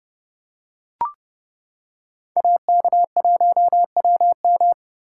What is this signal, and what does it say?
Morse code, heard as a keyed tone, about 28 wpm, 700 Hz. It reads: AK1WM